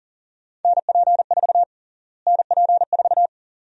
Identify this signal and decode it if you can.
Morse code, heard as a keyed tone, about 40 wpm, 700 Hz. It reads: NP4 NP4
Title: USA Call Sign Prefixes - Repeated 2x - 40wpm